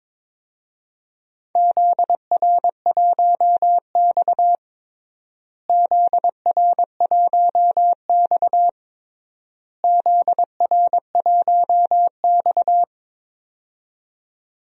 Morse code, heard as a keyed tone, about 22 wpm, 700 Hz. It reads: ZR1X ZR1X ZR1X
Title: DXCC-100 Call Signs - 4 Characters - Repeated 3x - 22wpm